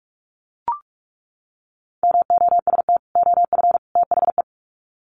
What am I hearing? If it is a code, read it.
Morse code, heard as a keyed tone, about 44 wpm, 700 Hz. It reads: MOST OF THE
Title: Sets of 3 Words 45wpm